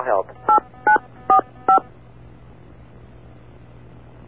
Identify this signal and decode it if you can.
DTMF, heard as telephone touch tones, tones of about 110 ms, with speech. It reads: 8945